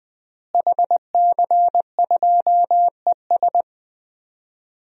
Morse code, heard as a keyed tone, about 20 wpm, 700 Hz. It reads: HC2ES
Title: DXCC-100 Call Signs - 5 Characters - Repeated 3x - 20wpm